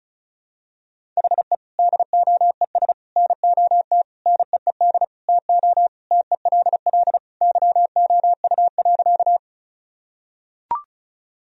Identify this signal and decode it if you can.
Morse code, heard as a keyed tone, about 36 wpm, 700 Hz. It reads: HE DOES NOT NEED TO TELL YOU.
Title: Sentences from Top 1000 Words - Review Part 1 - 35wpm